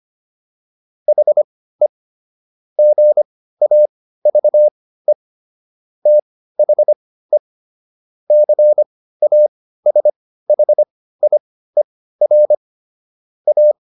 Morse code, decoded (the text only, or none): HE GAVE THE CASHIER A